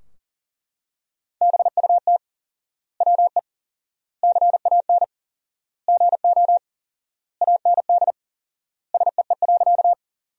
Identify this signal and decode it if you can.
Morse code, heard as a keyed tone, about 40 wpm, 700 Hz. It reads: BUT WE CAN GO AND SEE.
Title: 4x Word-Spacing - Sentences from Top 100 Words - 40wpm